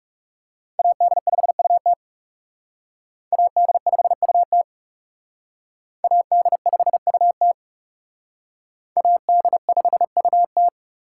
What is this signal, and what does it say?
Morse code, heard as a keyed tone, about 39 wpm, 700 Hz. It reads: AD5UT AD5UT AD5UT AD5UT